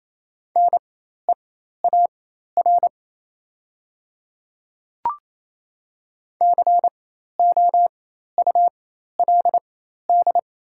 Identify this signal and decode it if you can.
Morse code, heard as a keyed tone, about 28 wpm, spaced as for 13 wpm, 700 Hz. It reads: NEAR COULD
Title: 4x Character Spacing - Top 200 Words - 28wpm